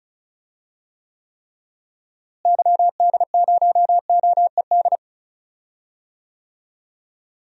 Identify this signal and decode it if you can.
Morse code, heard as a keyed tone, about 36 wpm, 700 Hz. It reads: YD0OED
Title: DXCC-100 Call Signs - 6 Characters - Repeated 3x - 35wpm